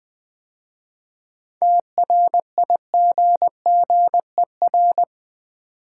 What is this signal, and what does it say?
Morse code, heard as a keyed tone, about 20 wpm, 700 Hz. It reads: TRIGGER
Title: 7 Letter Words - First 3 Letters Given - 20wpm